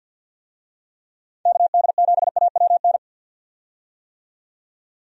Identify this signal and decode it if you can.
Morse code, heard as a keyed tone, about 53 wpm, 700 Hz. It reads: KD7AWN
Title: USA Call Signs - 6 Characters - 50wpm